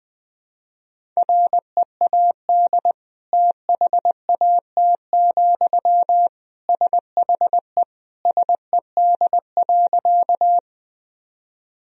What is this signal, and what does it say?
Morse code, heard as a keyed tone, about 21 wpm, 700 Hz. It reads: READ THAT, SHE SED.
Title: Sentences from Top 200 Words and Abbreviations 20wpm